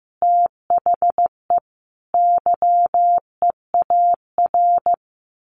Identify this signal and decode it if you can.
Morse code, heard as a keyed tone, about 15 wpm, 700 Hz. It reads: THE YEAR
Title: Sets of 5 Words Encore2 15wpm